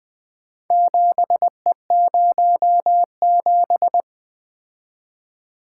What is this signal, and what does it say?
Morse code, heard as a keyed tone, about 24 wpm, 700 Hz. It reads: 7E07
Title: Speed-Racing - Four Letter-Number Combinations - 20wpm